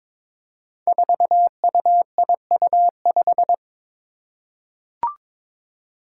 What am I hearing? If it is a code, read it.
Morse code, heard as a keyed tone, about 22 wpm, 700 Hz. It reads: 4UIU5
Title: Five Challenging Letter-Number Combinations 22wpm